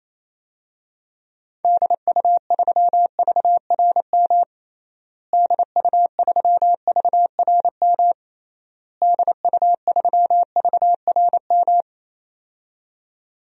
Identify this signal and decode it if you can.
Morse code, heard as a keyed tone, about 28 wpm, 700 Hz. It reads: DU3VRM DU3VRM DU3VRM
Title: DXCC-100 Call Signs - 6 Characters - Repeated 3x - 28wpm